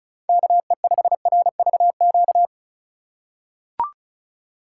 Morse code, heard as a keyed tone, about 34 wpm, 700 Hz. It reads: KE5RVQ
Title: USA Call Signs - 6 Characters - 2x Character Spacing - 35wpm